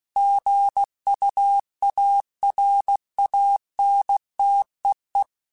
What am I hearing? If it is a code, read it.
Morse code, heard as a keyed tone, about 16 wpm, 800 Hz. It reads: GUARANTEE
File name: Quotes / Quotes-mp3s / Airtight @15